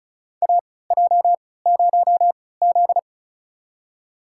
Morse code, heard as a keyed tone, about 34 wpm, spaced as for 20 wpm, 700 Hz. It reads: AJ0Z
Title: USA Call Signs - 4 Characters - 3x Character Spacing - 35wpm